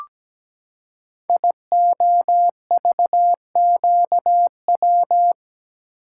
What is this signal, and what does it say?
Morse code, heard as a keyed tone, about 17 wpm, 700 Hz. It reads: IOVQW
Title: Code Groups - 5 Letter:Numbers - 17wpm